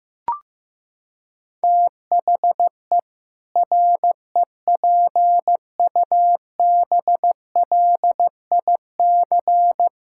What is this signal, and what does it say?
Morse code, heard as a keyed tone, about 15 wpm, 700 Hz. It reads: THE REPUBLIC
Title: Sentences from Top 1600 Words 15wpm